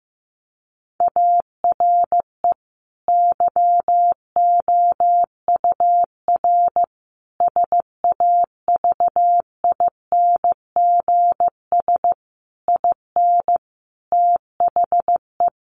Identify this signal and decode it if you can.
Morse code, heard as a keyed tone, about 15 wpm, 700 Hz. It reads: ARE YOUR SAVINGS IN THE
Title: Sentences from Top 1200 Words 15wpm